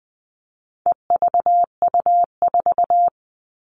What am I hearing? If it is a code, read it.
Morse code, heard as a keyed tone, about 20 wpm, 700 Hz. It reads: EVU4
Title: All Cut Numbers - 4-Digit 20wpm